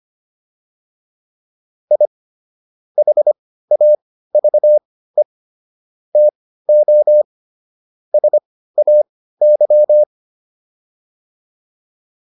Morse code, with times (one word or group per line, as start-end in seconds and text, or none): 1.9-2.1 I
3.0-5.2 HAVE
6.2-7.2 TO
8.1-10.0 SAY